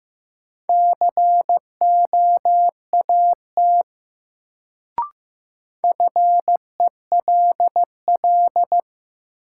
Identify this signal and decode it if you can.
Morse code, heard as a keyed tone, about 15 wpm, 700 Hz. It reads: COAT FELL
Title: Sentences from Top 1600 Words - New Word Drill Alt - 15wpm